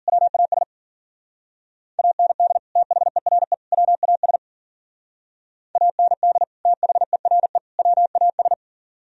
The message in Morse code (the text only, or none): WAS AND THERE WAS AND THERE WAS